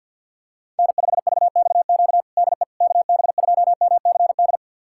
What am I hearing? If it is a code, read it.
Morse code, heard as a keyed tone, about 50 wpm, 700 Hz. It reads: N5VXQ DE KB2MCD
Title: USA Callsign DE Callsign 50wpm